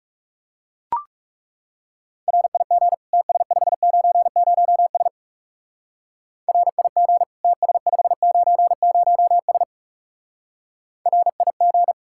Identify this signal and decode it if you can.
Morse code, heard as a keyed tone, about 36 wpm, 700 Hz. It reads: RIG TS590S RIG TS590S RIG
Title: Speed-Racing Common QSO Phrases Encore - 30wpm